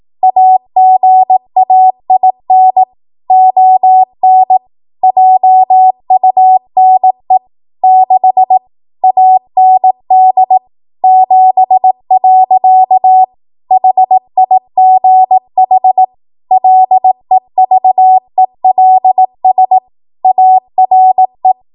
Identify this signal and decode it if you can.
Morse code, heard as a keyed tone, about 18 wpm, 750 Hz. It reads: AGAIN ON JUNE 6 AND 7. HIGH LEVELS ARE
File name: Morse / 18WPM